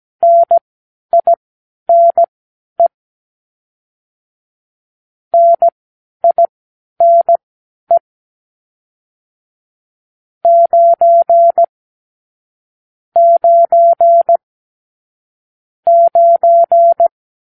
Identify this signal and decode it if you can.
Morse code, heard as a keyed tone, about 17 wpm, spaced as for 11 wpm, 680 Hz. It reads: NINE NINE 9 9 9
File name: k7qo / Lesson 55 (1-9)